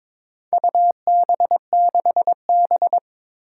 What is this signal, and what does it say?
Morse code, heard as a keyed tone, about 22 wpm, 700 Hz. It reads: UB6B
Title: Four Challenging Letter-Number Combinations 22wpm